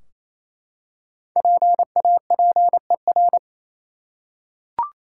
Morse code, heard as a keyed tone, about 28 wpm, 700 Hz. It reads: PAPER
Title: Five Letter Words - 28wpm